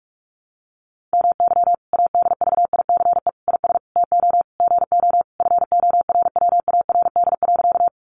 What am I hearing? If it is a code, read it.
Morse code, heard as a keyed tone, about 44 wpm, 700 Hz. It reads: MY ADVICE IS TO GO FORWARD.
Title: Sentences from Top 1300 Words 45wpm